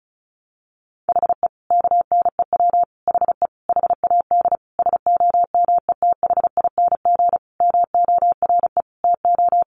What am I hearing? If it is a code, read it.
Morse code, heard as a keyed tone, about 34 wpm, 700 Hz. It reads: HE KNEW HE HAD SOMETHING MORE TO
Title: Sentences from Top 600 Words 35wpm